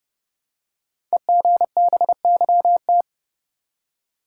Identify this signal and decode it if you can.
Morse code, heard as a keyed tone, about 30 wpm, 700 Hz. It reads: EGBYT
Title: Code Groups - 5 Letter:Numbers - 30wpm